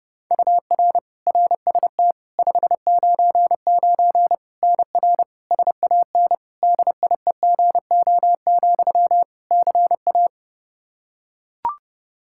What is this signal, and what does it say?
Morse code, heard as a keyed tone, about 30 wpm, 700 Hz. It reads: UR RST 599 NR SAN DIEGO, CA
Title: Practice QSO - Exchange 1 - Beginning - 30wpm